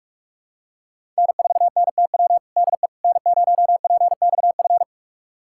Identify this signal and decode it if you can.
Morse code, heard as a keyed tone, about 46 wpm, 700 Hz. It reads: N4NTW DE N0PXF
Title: USA Callsign DE Callsign - 5 Characters - 45wpm